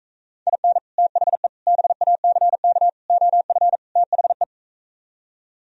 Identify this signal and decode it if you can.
Morse code, heard as a keyed tone, about 44 wpm, 700 Hz. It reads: IN THE BACK OF THE